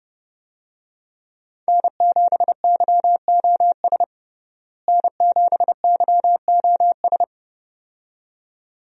Morse code, heard as a keyed tone, about 29 wpm, 700 Hz. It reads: N7YOS N7YOS
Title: USA Call Signs - 5 Characters - Repeated 2x - 30wpm